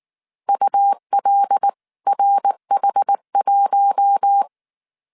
Morse code, heard as a keyed tone, about 19 wpm, 800 Hz. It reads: ULRH1
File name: CW / DET / 1 6 1500C S C 18wpm 2356